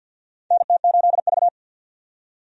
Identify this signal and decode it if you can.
Morse code, heard as a keyed tone, about 49 wpm, 700 Hz. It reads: NT8V
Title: USA Call Signs - 4 Characters - 50wpm